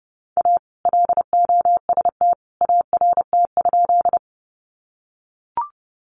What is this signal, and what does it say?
Morse code, heard as a keyed tone, about 30 wpm, 700 Hz. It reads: A LOST ART?
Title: Sentences from Top 1400 Words 30wpm